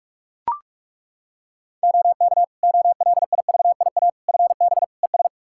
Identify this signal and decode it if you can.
Morse code, heard as a keyed tone, about 42 wpm, 700 Hz. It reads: OK OLIVIA FB ES
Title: Practice QSO - Exchange 2 - Info - 45wpm